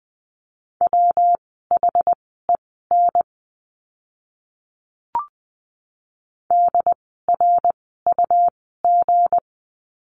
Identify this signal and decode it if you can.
Morse code, heard as a keyed tone, about 20 wpm, spaced as for 15 wpm, 700 Hz. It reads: WHEN DRUG
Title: 160 Lesson 26f - Letter g - Words 20wpm